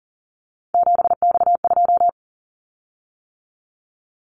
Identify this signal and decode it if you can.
Morse code, heard as a keyed tone, about 40 wpm, 700 Hz. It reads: 7X2
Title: Callsign Prefixes for Top 100 DXCC - 40wpm